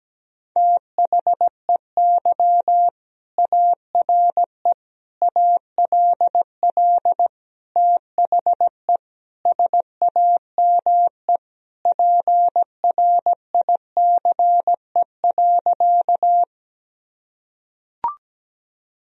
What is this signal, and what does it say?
Morse code, heard as a keyed tone, about 17 wpm, 700 Hz. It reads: THEY ARE ALL THE SAME PRICE.
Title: Sentences from Top 1000 Words - Review Part 4 - 17wpm